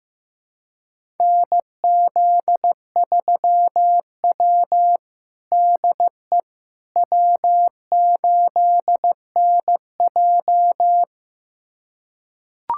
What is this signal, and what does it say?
Morse code, heard as a keyed tone, about 15 wpm, 700 Hz. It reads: NZ3W DE W8NJ
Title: USA Callsign DE Callsign - 4 Characters - 15wpm